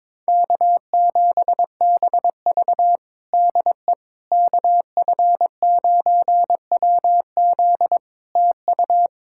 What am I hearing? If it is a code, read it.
Morse code, heard as a keyed tone, about 22 wpm, 700 Hz. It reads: K7BV DE KF9WZ TU
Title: K1USN SST QSO Practice 22wpm